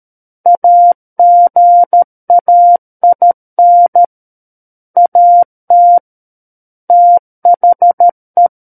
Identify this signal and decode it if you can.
Morse code, heard as a keyed tone, about 13 wpm, 700 Hz. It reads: AGAIN AT THE
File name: Book One / 04-Chapter 4 - 13 WPM